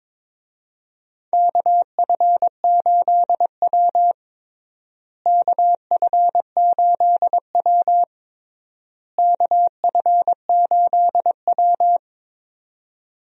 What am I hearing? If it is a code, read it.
Morse code, heard as a keyed tone, about 23 wpm, 700 Hz. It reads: KF8W KF8W KF8W
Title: USA Call Signs - 4 Characters - Repeated 3x - 22wpm